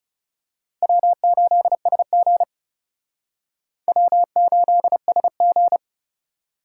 Morse code, heard as a keyed tone, about 34 wpm, 700 Hz. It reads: W8SG W8SG
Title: Speed Racing - USA 1x2 Call Signs 30wpm